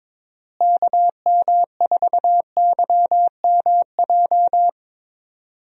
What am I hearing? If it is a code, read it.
Morse code, heard as a keyed tone, about 22 wpm, 700 Hz. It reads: KM4YMJ
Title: USA Call Signs - 6 Characters - Repeated 2x - 22wpm